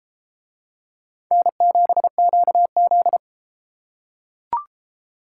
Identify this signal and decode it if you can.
Morse code, heard as a keyed tone, about 31 wpm, 700 Hz. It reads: N7QZ